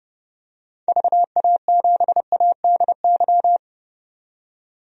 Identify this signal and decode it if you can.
Morse code, heard as a keyed tone, about 30 wpm, 700 Hz. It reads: VA7ADY
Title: DXCC-100 Call Signs - 6 Characters - 30wpm